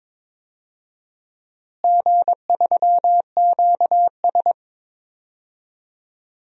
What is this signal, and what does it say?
Morse code, heard as a keyed tone, about 22 wpm, 700 Hz. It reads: G3QS
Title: DXCC-100 Call Signs - 4 Characters - Repeated 3x - 22wpm